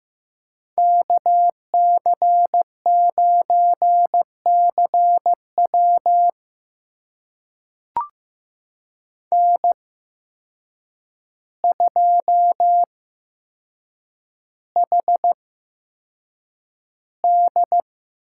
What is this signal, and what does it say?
Morse code, heard as a keyed tone, about 15 wpm, spaced as for 7 wpm, 700 Hz. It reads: KC9CW N2HD
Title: USA Call Signs - 5 Characters - 8x Character Spacing - 15wpm